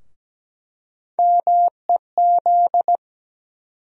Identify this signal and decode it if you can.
Morse code, heard as a keyed tone, about 17 wpm, 700 Hz. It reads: MEZ
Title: USA Call Sign Postfixes - 3 Letters 17wpm